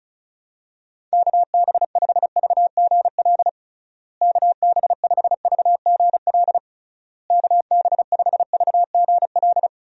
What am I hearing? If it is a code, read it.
Morse code, heard as a keyed tone, about 34 wpm, 700 Hz. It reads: KB5VGL KB5VGL KB5VGL
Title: USA Call Signs - 6 Characters - Repeated 3x - 35wpm